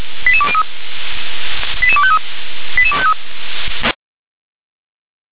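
Five-tone selective call (ZVEI), heard as quick runs of five tones, tones of about 70 ms. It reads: 88203 88253 88253
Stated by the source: Received on SDR as a ZVEI five-tone selective call, code 88036.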